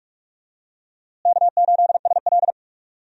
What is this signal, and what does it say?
Morse code, heard as a keyed tone, about 47 wpm, 700 Hz. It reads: K8SL
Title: USA Call Signs - 4 Characters - Repeated 2x - 45wpm